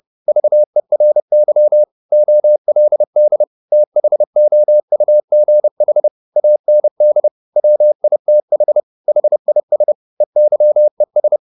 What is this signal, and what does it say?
Morse code, heard as a keyed tone, about 30 wpm, 600 Hz. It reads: VERY OLD THOUGH AND WITH HIS EYES
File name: the old man and the sea / the old man and the sea 1 30wpm